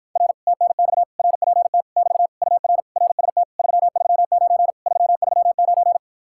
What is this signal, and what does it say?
Morse code, heard as a keyed tone, about 54 wpm, 700 Hz. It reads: R TNX RPT <BT> UR RST 339 339